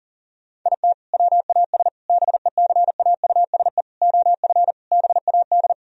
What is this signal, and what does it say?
Morse code, heard as a keyed tone, about 39 wpm, 700 Hz. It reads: IT WAS BECAUSE OF BAD
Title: Sentences from Top 1000 Words - Review Part 4 - 40wpm